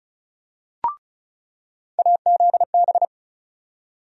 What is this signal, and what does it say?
Morse code, heard as a keyed tone, about 36 wpm, 700 Hz. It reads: AZB
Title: USA Call Sign Postfixes - 3 Letters 35wpm